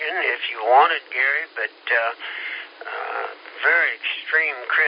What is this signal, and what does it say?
Emergency Alert System (SAME) tones, heard as harsh buzzing bursts, with speech. It is heard from 0.9 to 4.0 s.